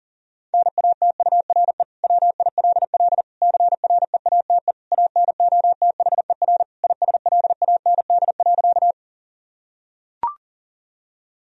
Morse code, heard as a keyed tone, about 37 wpm, 700 Hz. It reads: NATURE WILL CREATE ANOTHER ISLAND.
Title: Sentences from Top 1000 Words - Review Part 3 - 40wpm